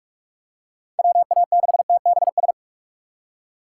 Morse code, heard as a keyed tone, about 46 wpm, 700 Hz. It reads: WA6TBS
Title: USA Call Signs - 4x Character Spacing - 45wpm